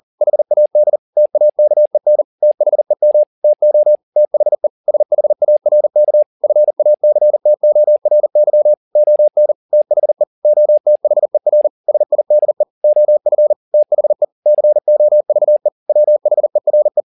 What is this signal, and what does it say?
Morse code, heard as a keyed tone, about 40 wpm, 600 Hz. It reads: HAD TAKEN THEM TO THE SHARK FACTORY ON THE OTHER SIDE OF THE COVE WHERE
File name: the old man and the sea 1 40wpm